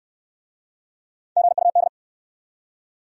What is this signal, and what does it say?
Morse code, heard as a keyed tone, about 59 wpm, 700 Hz. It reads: GUD